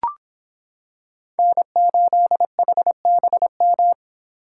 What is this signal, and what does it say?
Morse code, heard as a keyed tone, about 26 wpm, 700 Hz. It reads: N8HBM